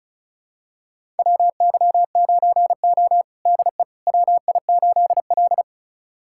Morse code, heard as a keyed tone, about 36 wpm, 700 Hz. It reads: WY9O DE WI8L
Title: USA Callsign DE Callsign - 4 Characters - 35wpm